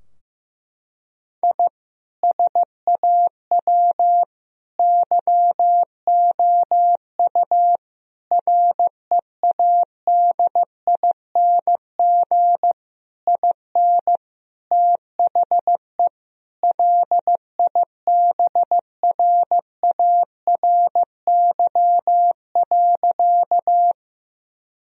Morse code, heard as a keyed tone, about 15 wpm, 700 Hz. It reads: I SAW YOU READING IN THE LIBRARY.
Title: Sentences from Top 1000 Words - Review Part 4 - 15wpm